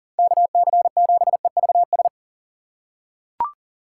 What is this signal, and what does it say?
Morse code, heard as a keyed tone, about 39 wpm, 700 Hz. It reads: KC7EVS
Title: USA Call Signs - 6 Characters - 2x Character Spacing - 40wpm